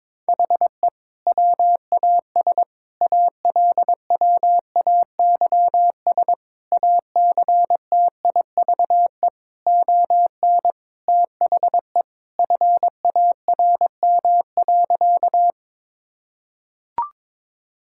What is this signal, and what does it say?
Morse code, heard as a keyed tone, about 22 wpm, 700 Hz. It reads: HE WAS ALWAYS ACTIVE ON THE FARM.
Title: Sentences from Top 1000 Words - Review Part 6 - 22wpm